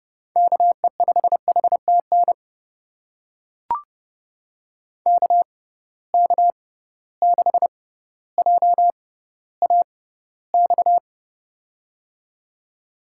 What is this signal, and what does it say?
Morse code, heard as a keyed tone, about 31 wpm, spaced as for 16 wpm, 700 Hz. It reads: KE5HTN KK6JAX
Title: USA Call Signs - 6x Character Spacing - 30wpm